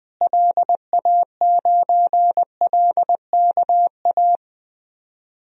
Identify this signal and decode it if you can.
Morse code, heard as a keyed tone, about 20 wpm, 700 Hz. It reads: LA9LKA
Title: DXCC-100 Call Signs - 6 Characters - Repeated 2x - 20wpm